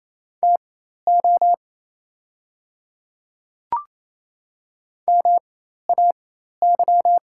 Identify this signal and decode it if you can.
Morse code, heard as a keyed tone, about 28 wpm, spaced as for 13 wpm, 690 Hz. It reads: TO MAY
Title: 4x Character Spacing - Top 100 Words - 28wpm